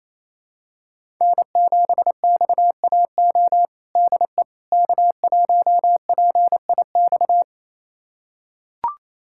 Morse code, heard as a keyed tone, about 28 wpm, 700 Hz. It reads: N7XAO DE K1PIX
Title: USA Callsign DE Callsign - 5 Characters - 28wpm